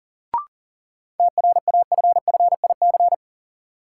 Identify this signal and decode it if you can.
Morse code, heard as a keyed tone, about 40 wpm, 700 Hz. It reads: TRAFFIC
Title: Sentences from Top 1300 Words - New Word Drill - 40wpm